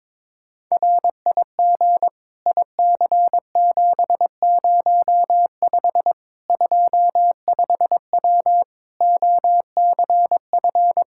Morse code, heard as a keyed tone, about 22 wpm, 700 Hz. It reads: RIG IC705 25W OCF
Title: K1USN SST QSO Practice 22wpm